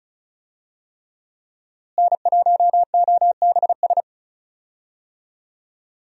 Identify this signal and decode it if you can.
Morse code, heard as a keyed tone, about 34 wpm, 700 Hz. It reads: N1OBS
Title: USA Call Signs - 5 Characters - Repeated 3x - 35wpm